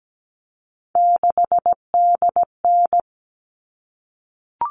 Morse code, heard as a keyed tone, about 17 wpm, 700 Hz. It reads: 6DN
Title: All Cut Numbers - 3-Digit 17wpm